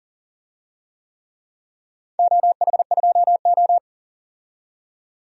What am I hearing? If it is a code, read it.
Morse code, heard as a keyed tone, about 41 wpm, 700 Hz. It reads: OH2O
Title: DXCC-100 Call Signs - 4 Characters - 40wpm